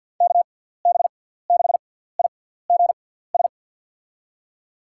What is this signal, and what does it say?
Morse code, heard as a keyed tone, about 47 wpm, spaced as for 17 wpm, 700 Hz. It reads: KB6IGS
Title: USA Call Signs - 6x Character Spacing - 50wpm